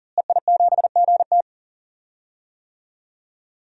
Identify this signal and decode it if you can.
Morse code, heard as a keyed tone, about 40 wpm, 700 Hz. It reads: EI7GT
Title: DXCC-100 Call Signs - 5 Characters - Repeated 3x - 40wpm